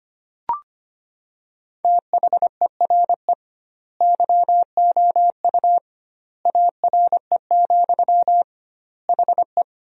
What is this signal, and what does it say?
Morse code, heard as a keyed tone, about 25 wpm, 700 Hz. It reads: THERE YOU ARE, HE
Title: 2x Word-Spacing - Sentences from Top 100 Words - 25wpm